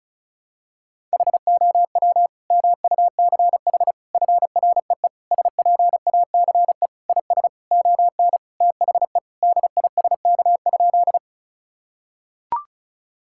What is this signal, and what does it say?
Morse code, heard as a keyed tone, about 34 wpm, 700 Hz. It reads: HOW MUCH FREE SPACE IS ON THE DISK?
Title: Sentences from Top 1000 Words - Review Part 6 - 35wpm